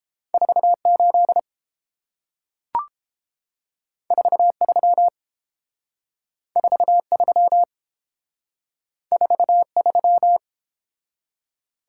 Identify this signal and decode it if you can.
Morse code, heard as a keyed tone, about 30 wpm, 700 Hz. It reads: 48 43 43 43